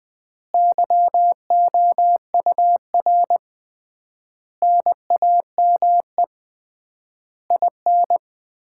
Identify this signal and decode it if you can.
Morse code, heard as a keyed tone, about 20 wpm, 700 Hz. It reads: YOUR NAME IN
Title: Common QSO Phrases - 3x Word-Spacing - 20wpm